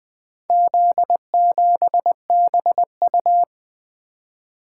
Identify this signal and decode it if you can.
Morse code, heard as a keyed tone, about 20 wpm, 700 Hz. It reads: Z7BU
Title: Four Letter-Number Combinations - 20wpm